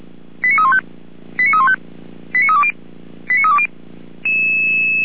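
Five-tone selective call (ZVEI), heard as quick runs of five tones, tones of about 70 ms. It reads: 89315 89315 89320 89320
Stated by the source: ZVEI five-tone selective call 89329 received by SDR, from fire brigade Feuerwehr Orlamünde's audio server.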